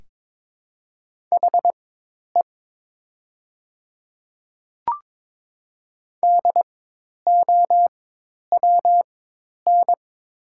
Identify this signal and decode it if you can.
Morse code, heard as a keyed tone, about 22 wpm, spaced as for 10 wpm, 700 Hz. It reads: HE DOWN
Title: 4x Character Spacing - Top 100 Words - 22wpm